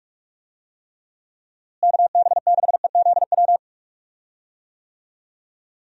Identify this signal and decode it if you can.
Morse code, heard as a keyed tone, about 42 wpm, 700 Hz. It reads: KB6EZW
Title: USA Call Signs - 6 Characters - Repeated 2x - 45wpm